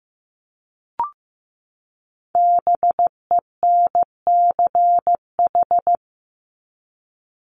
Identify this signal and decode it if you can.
Morse code, heard as a keyed tone, about 15 wpm, 700 Hz. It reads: BENCH
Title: Top 1251-1500 Words 15wpm